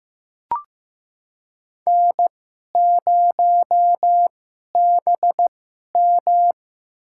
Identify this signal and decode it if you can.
Morse code, heard as a keyed tone, about 15 wpm, spaced as for 11 wpm, 690 Hz. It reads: N0BM
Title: USA Call Signs - 4 Characters - 2x Character Spacing - 15wpm